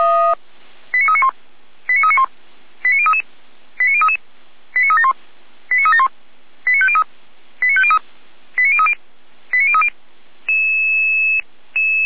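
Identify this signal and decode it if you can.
Five-tone selective call (ZVEI), heard as quick runs of five tones, tones of about 70 ms. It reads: 89391 89391 89033 89033 89371 89371 89503 89503 89030 89030
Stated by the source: ZVEI five-tone selective call received by SDR.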